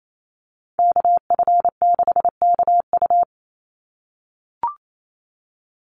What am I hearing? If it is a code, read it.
Morse code, heard as a keyed tone, about 28 wpm, 700 Hz. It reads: KF6KU
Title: USA Call Signs 28wpm